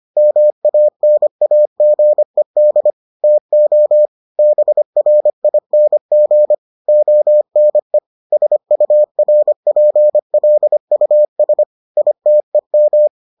Morse code, heard as a keyed tone, about 25 wpm, 600 Hz. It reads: MANAGED TO BRING ONE SURPLUS ITEM